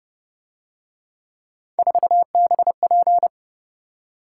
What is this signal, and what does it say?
Morse code, heard as a keyed tone, about 31 wpm, 700 Hz. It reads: 4BP
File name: Speed-Racing - Three Letter-Number Combinations - 25wpm